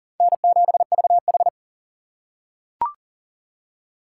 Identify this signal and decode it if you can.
Morse code, heard as a keyed tone, about 40 wpm, 700 Hz. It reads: N7VH
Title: USA Call Signs - 4 Characters - 2x Character Spacing - 40wpm